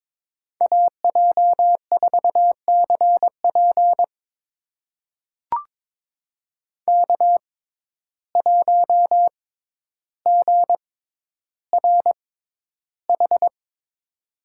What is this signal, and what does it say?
Morse code, heard as a keyed tone, about 22 wpm, spaced as for 11 wpm, 700 Hz. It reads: AJ4CP K1GRH
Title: USA Call Signs - 5 Characters - 6x Character Spacing - 22wpm